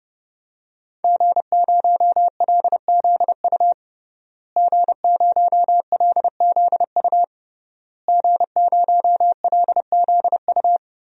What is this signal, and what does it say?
Morse code, heard as a keyed tone, about 31 wpm, 700 Hz. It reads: G0LZU G0LZU G0LZU
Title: DXCC-100 Call Signs - 5 Characters - Repeated 3x - 30wpm